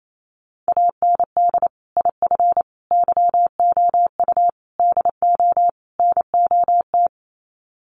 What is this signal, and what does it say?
Morse code, heard as a keyed tone, about 28 wpm, 700 Hz. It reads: AND IF YOU DO NOT
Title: Sets of 5 Words 28wpm